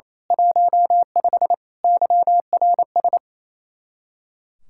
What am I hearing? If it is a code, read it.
Morse code, heard as a keyed tone, about 27 wpm, 700 Hz. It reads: 15 YRS
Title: Speed-Racing Common QSO Phrases - 28wpm